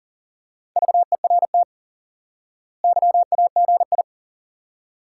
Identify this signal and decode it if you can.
Morse code, heard as a keyed tone, about 37 wpm, 700 Hz. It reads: VERT YAGI
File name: IQE Course 3 - Lesson 89 - YAGI - Two Elements - 40wpm